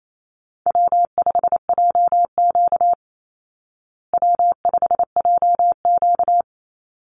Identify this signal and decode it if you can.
Morse code, heard as a keyed tone, about 28 wpm, 700 Hz. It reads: W5JQ W5JQ
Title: Call Signs Repeated 2x - 28wpm